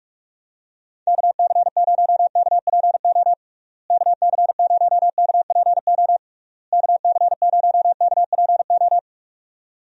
Morse code, heard as a keyed tone, about 41 wpm, 700 Hz. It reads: KC0KPO KC0KPO KC0KPO
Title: USA Call Signs - 6 Characters - Repeated 3x - 45wpm